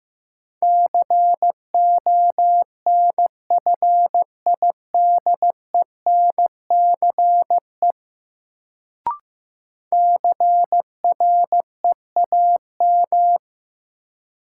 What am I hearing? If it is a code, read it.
Morse code, heard as a keyed tone, about 15 wpm, 700 Hz. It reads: CONFIDENCE CREAM
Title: Sentences from Top 1500 Words - New Word Drill - 15wpm